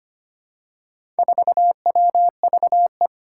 Morse code, heard as a keyed tone, about 25 wpm, 700 Hz. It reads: 4WVE
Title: Four Challenging Letter-Number Combinations 25wpm